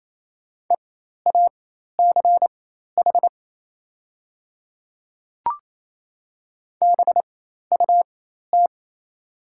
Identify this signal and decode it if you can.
Morse code, heard as a keyed tone, about 28 wpm, spaced as for 13 wpm, 700 Hz. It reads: EACH BUT
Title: 4x Character Spacing - Top 200 Words - 28wpm